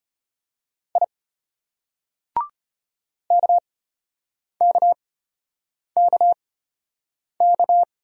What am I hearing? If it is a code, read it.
Morse code, heard as a keyed tone, about 32 wpm, spaced as for 8 wpm, 700 Hz. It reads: I KKKK